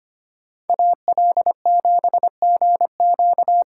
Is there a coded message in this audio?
Yes: Morse code from 0.7 to 3.6 s, heard as a keyed tone, about 33 wpm, 700 Hz. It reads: AL7GQ